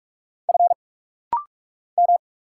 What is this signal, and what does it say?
Morse code, heard as a keyed tone, about 46 wpm, 700 Hz. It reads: F M
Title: Single Letters - Rapid-Fire 45wpm